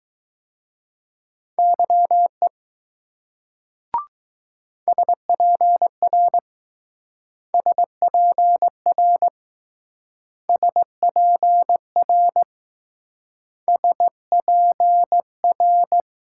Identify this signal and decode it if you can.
Morse code, heard as a keyed tone, about 20 wpm, 700 Hz. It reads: YE SPR SPR SPR SPR